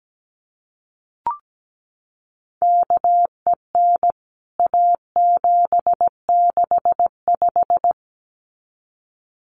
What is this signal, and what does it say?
Morse code, heard as a keyed tone, about 17 wpm, 700 Hz. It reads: KEN A765
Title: CWT Exchange 17wpm